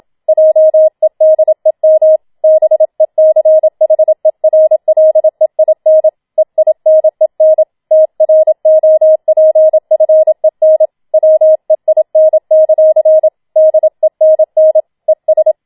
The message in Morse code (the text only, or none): JEDEM BECHERLEIN EINEN TROPFEN WEIN; DENN ES